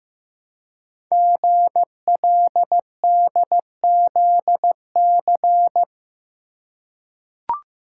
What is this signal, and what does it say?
Morse code, heard as a keyed tone, about 15 wpm, 700 Hz. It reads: GLDZC
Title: Code Groups - 5 Letters - 15wpm